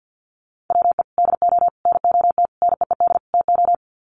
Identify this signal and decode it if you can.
Morse code, heard as a keyed tone, about 50 wpm, 700 Hz. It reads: WE DO NOT NEED TO
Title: Sets of 5 Words Encore 50wpm